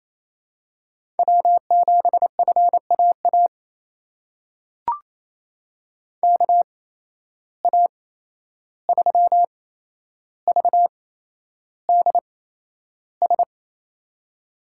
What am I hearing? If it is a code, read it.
Morse code, heard as a keyed tone, about 28 wpm, spaced as for 11 wpm, 700 Hz. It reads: W7FAA KA3VDS
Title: USA Call Signs - 8x Character Spacing - 28wpm